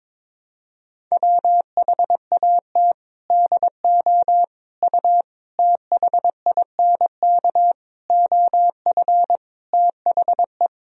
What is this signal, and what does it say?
Morse code, heard as a keyed tone, about 22 wpm, 700 Hz. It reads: WHAT DO U THINK OF THE
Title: Sentences from Top 300 Words and Abbreviations 22wpm